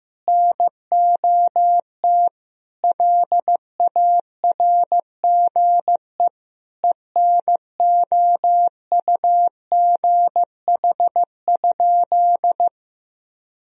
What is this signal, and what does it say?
Morse code, heard as a keyed tone, about 15 wpm, 700 Hz. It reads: NOT LARGE ENOUGH?
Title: Sentences from Top 1000 Words - Review Part 1 - 15wpm